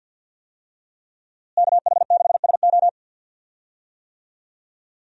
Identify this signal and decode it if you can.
Morse code, heard as a keyed tone, about 48 wpm, 700 Hz. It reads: KH6SO
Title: USA Call Signs - 5 Characters - Repeated 3x - 50wpm